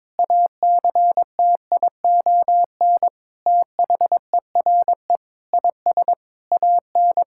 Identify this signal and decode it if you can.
Morse code, heard as a keyed tone, about 22 wpm, 700 Hz. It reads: ACTION THERE IS AN
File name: practice / 22wpm / Sentences with 8-Word Phrases - 22wpm